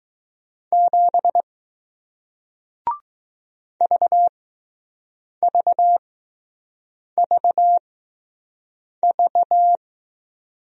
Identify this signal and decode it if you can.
Morse code, heard as a keyed tone, about 20 wpm, spaced as for 6 wpm, 700 Hz. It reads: 7 VVVV